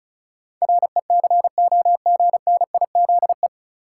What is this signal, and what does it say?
Morse code, heard as a keyed tone, about 36 wpm, 700 Hz. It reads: RECOGNIZE
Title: Sentences from Top 1400 Words - New Word Drill Alt - 35wpm